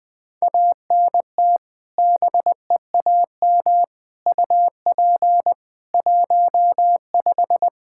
Morse code, heard as a keyed tone, about 21 wpm, 700 Hz. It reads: ANT BEAM UP 15
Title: Practice QSO - Exchange 2 - Info - 20wpm